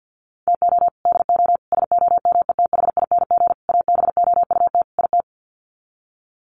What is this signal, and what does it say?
Morse code, heard as a keyed tone, about 50 wpm, 700 Hz. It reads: TO DO SOMETHING ABOUT IT
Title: Sets of 5 Words 50wpm